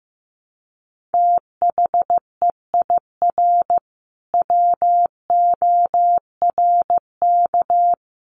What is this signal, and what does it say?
Morse code, heard as a keyed tone, about 15 wpm, 700 Hz. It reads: THEIR WORK
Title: Sentences from Top 1500 Words 15wpm